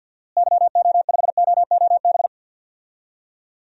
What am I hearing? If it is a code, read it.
Morse code, heard as a keyed tone, about 52 wpm, 700 Hz. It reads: YO5OOB
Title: DXCC-100 Call Signs - 6 Characters - Repeated 3x - 50wpm